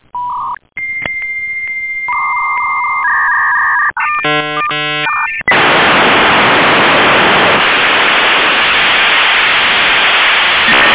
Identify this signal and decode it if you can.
Dial-up modem handshake.